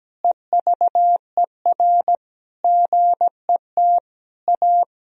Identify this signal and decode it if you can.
Morse code, heard as a keyed tone, about 17 wpm, 700 Hz. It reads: EVER GET A
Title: Sentences from Top 1000 Words - Review Part 3 - 17wpm